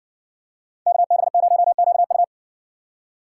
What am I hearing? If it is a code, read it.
Morse code, heard as a keyed tone, about 60 wpm, 700 Hz. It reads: KB0QU